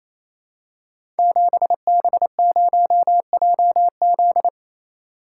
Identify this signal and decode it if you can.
Morse code, heard as a keyed tone, about 27 wpm, 700 Hz. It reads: 7B0JZ
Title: Code Groups - 5 Letter:Numbers - 28wpm